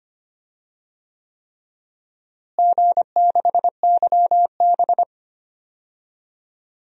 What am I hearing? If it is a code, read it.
Morse code, heard as a keyed tone, about 25 wpm, 700 Hz. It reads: G6YB
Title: DXCC-100 Call Signs - 4 Characters - Repeated 3x - 25wpm